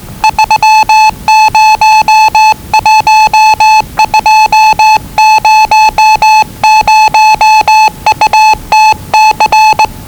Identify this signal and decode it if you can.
Morse code, heard as a keyed tone, about 20 wpm, 870 Hz. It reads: 301200UTC
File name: WellingtonRadioZlwCloseDownOnHfAndMf / 01-Zlw 4277khz 417.5 qrt until 0925z